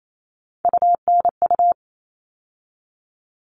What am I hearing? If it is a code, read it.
Morse code, heard as a keyed tone, about 28 wpm, 700 Hz. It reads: UNU
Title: All Cut Numbers - 3-Digit 28wpm